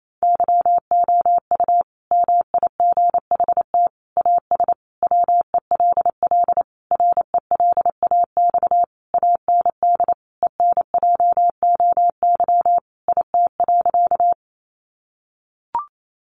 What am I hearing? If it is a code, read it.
Morse code, heard as a keyed tone, about 28 wpm, 700 Hz. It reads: YOU MIGHT AS WELL RELAX AND ENJOY IT.
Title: Sentences from Top 1400 Words 28wpm